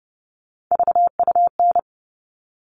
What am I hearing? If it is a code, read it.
Morse code, heard as a keyed tone, about 30 wpm, 700 Hz. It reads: VUN
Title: All Cut Numbers - 3-Digit 30wpm